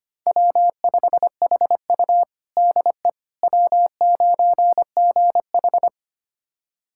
Morse code, heard as a keyed tone, about 25 wpm, 700 Hz. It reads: W5HU DE W9GH
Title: USA Callsign DE Callsign - 4 Characters - 25wpm